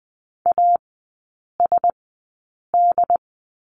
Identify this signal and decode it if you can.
Morse code, heard as a keyed tone, about 20 wpm, spaced as for 8 wpm, 700 Hz. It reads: ASD
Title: 067 Lesson 12e - Letter d - Three Characters 20wpm